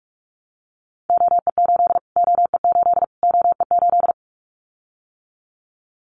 Morse code, heard as a keyed tone, about 44 wpm, 700 Hz. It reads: OE8 OE8 OE8
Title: Callsign Prefixes for Top 25 DXCC - Repeated 3x - 45wpm